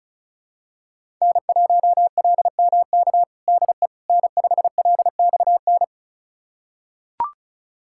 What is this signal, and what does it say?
Morse code, heard as a keyed tone, about 36 wpm, 700 Hz. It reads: N1LMK DE N5LXN
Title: USA Callsign DE Callsign - 5 Characters - 35wpm